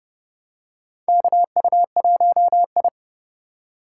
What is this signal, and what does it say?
Morse code, heard as a keyed tone, about 30 wpm, 700 Hz. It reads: KU1I
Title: USA Call Signs - 4 Characters - 2x Character Spacing - 30wpm